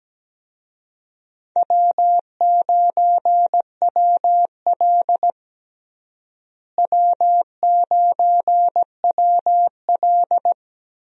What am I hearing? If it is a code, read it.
Morse code, heard as a keyed tone, about 17 wpm, 700 Hz. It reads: W9WL W9WL
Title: USA Call Signs - 4 Characters - Repeated 3x - 17wpm